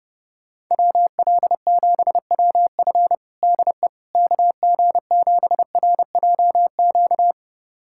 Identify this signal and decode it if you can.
Morse code, heard as a keyed tone, about 30 wpm, 700 Hz. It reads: WL7WF DE KG7RJQ
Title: USA Callsign DE Callsign 30wpm